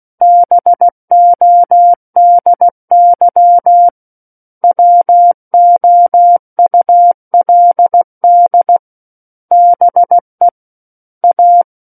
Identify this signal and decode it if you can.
Morse code, heard as a keyed tone, about 16 wpm, 700 Hz. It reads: BODY WOULD BE A
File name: Book One / 07-Chapter 7 - 16 WPM